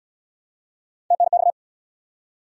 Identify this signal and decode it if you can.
Morse code, heard as a keyed tone, about 69 wpm, 700 Hz. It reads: TNX